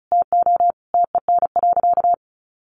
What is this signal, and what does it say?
Morse code, heard as a keyed tone, about 34 wpm, 700 Hz. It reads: TO TEN.
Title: Sentences from Top 1300 Words 35wpm